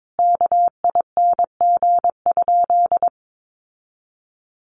Morse code, heard as a keyed tone, about 22 wpm, 700 Hz. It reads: KING?
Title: Sentences from Top 1400 Words 22wpm